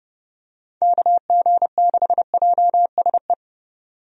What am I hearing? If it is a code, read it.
Morse code, heard as a keyed tone, about 30 wpm, 700 Hz. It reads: KG6JSE
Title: USA Call Signs - 6 Characters - 30wpm